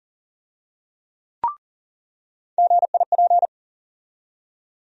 Morse code, heard as a keyed tone, about 40 wpm, 700 Hz. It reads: GIP
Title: USA Call Sign Postfixes - 3 Letters 40wpm